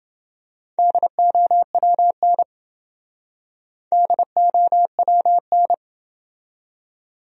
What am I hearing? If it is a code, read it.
Morse code, heard as a keyed tone, about 29 wpm, 700 Hz. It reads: DOWN DOWN